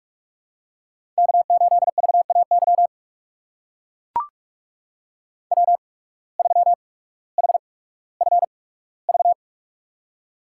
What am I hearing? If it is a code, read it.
Morse code, heard as a keyed tone, about 44 wpm, spaced as for 19 wpm, 700 Hz. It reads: K8VAY W3HFV
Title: USA Call Signs - 5 Characters - 8x Character Spacing - 45wpm